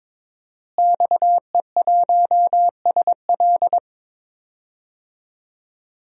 Morse code, heard as a keyed tone, about 22 wpm, 700 Hz. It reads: XE1SL
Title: DXCC-100 Call Signs - 5 Characters - Repeated 3x - 22wpm